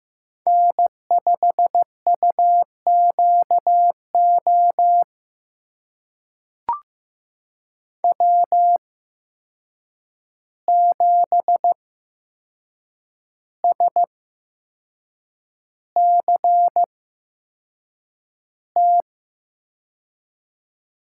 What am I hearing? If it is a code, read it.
Morse code, heard as a keyed tone, about 15 wpm, spaced as for 6 wpm, 700 Hz. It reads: N5UQO W7SCT
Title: USA Call Signs - 8x Character Spacing - 15wpm